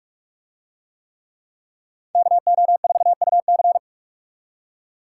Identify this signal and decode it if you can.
Morse code, heard as a keyed tone, about 46 wpm, 700 Hz. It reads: KO4UC